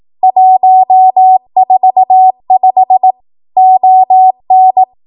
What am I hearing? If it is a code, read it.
Morse code, heard as a keyed tone, about 18 wpm, 750 Hz. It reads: 145 ON
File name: Archive / 18 WPM / 18WPM